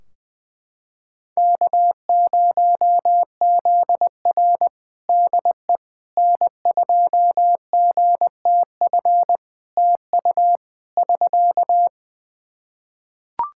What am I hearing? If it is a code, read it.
Morse code, heard as a keyed tone, about 20 wpm, 700 Hz. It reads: K0ZR DE N2GTF TU <SK>
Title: Practice QSO - Exchange 4 - Ending - 20wpm